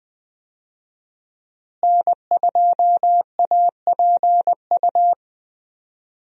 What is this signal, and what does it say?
Morse code, heard as a keyed tone, about 20 wpm, 700 Hz. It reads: N2APU
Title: USA Call Signs - 5 Characters - Repeated 2x - 20wpm